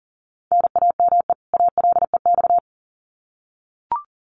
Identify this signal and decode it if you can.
Morse code, heard as a keyed tone, about 40 wpm, 700 Hz. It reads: NAME ALEX
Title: Common QSO Phrases 40wpm